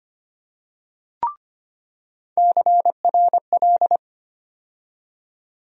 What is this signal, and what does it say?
Morse code, heard as a keyed tone, about 25 wpm, 700 Hz. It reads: CRL
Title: USA Call Sign Postfixes - 3 Letters 25wpm